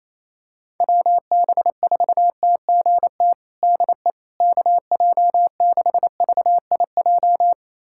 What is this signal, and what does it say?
Morse code, heard as a keyed tone, about 28 wpm, 700 Hz. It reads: WB4TGT DE KJ6VIJ
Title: USA Callsign DE Callsign - 6 Characters - 28wpm